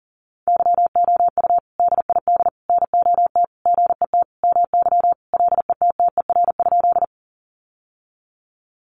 Morse code, heard as a keyed tone, about 40 wpm, 700 Hz. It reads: YOU DID NOT GET MY LETTER?
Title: Sentences from Top 600 Words 40wpm